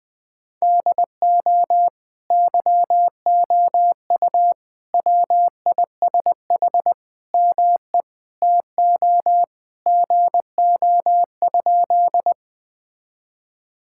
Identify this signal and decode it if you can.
Morse code, heard as a keyed tone, about 19 wpm, 700 Hz. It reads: DO YOU WISH ME TO GO?
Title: Sentences from Top 1600 Words 20wpm